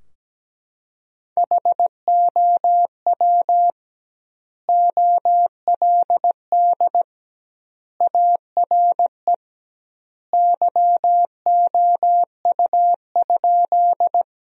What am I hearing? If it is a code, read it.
Morse code, heard as a keyed tone, about 17 wpm, 700 Hz. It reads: HOW OLD ARE YOU?
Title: 2x Word-Spacing - Sentences from Top 200 Words - 17wpm